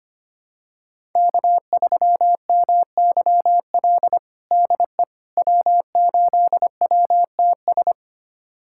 Morse code, heard as a keyed tone, about 25 wpm, 700 Hz. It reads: K3MYL DE W8WTS
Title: USA Callsign DE Callsign - 5 Characters - 25wpm